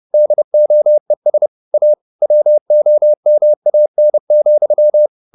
Morse code, heard as a keyed tone, about 30 wpm, 600 Hz. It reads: DOES A WOMAN,
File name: the old man and the sea / the old man and the sea 1 30wpm